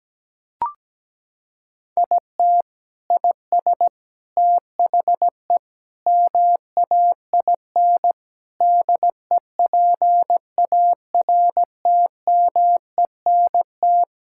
Morse code, heard as a keyed tone, about 17 wpm, 700 Hz. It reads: IT IS THE MAIN DEPARTMENT
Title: Sentences from Top 1000 Words - Review Part 5 - 17wpm